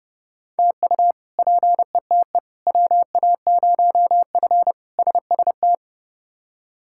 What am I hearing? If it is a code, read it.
Morse code, heard as a keyed tone, about 30 wpm, 700 Hz. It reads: TU PETE WA0F SST
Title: K1USN SST QSO Practice - 30wpm